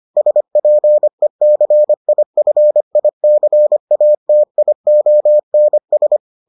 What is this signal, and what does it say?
Morse code, heard as a keyed tone, about 25 wpm, 600 Hz. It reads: SPECIFICATIONS